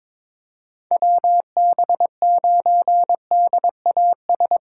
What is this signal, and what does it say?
Morse code, heard as a keyed tone, about 22 wpm, 700 Hz. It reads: WB9DAS
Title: USA Callsign DE Callsign - 6 Characters - 22wpm